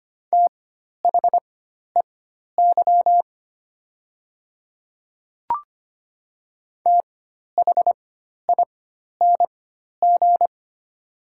Morse code, heard as a keyed tone, about 25 wpm, spaced as for 12 wpm, 700 Hz. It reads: THEY THING
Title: 4x Character Spacing - Top 100 Words - 25wpm